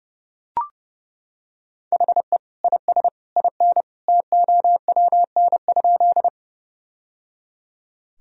Morse code, heard as a keyed tone, about 30 wpm, 700 Hz. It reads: HE IS IN TOWN?
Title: Sentences from Top 1000 Words - Review Part 2 - 30wpm